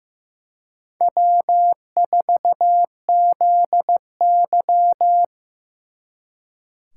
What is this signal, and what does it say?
Morse code, heard as a keyed tone, about 18 wpm, 700 Hz. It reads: W4ZY
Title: Speed Racing - USA Call Signs 15wpm